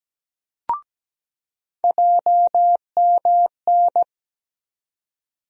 Morse code, heard as a keyed tone, about 17 wpm, 700 Hz. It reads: JMN
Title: USA Call Sign Postfixes - 3 Letters 17wpm